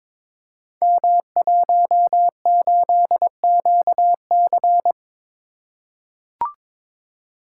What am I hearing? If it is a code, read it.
Morse code, heard as a keyed tone, about 22 wpm, 700 Hz. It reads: M18QC
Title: Code Groups - 5 Letter:Numbers - 22wpm